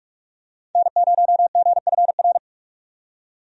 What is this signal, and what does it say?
Morse code, heard as a keyed tone, about 47 wpm, 700 Hz. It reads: N0GFR
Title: USA Call Signs - 5 Characters - 45wpm